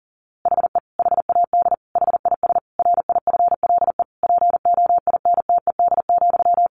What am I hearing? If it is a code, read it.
Morse code, heard as a keyed tone, about 40 wpm, 700 Hz. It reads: HE HAD HIS RIFLE POINTED,
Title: A Sound Of Thunder 40wpm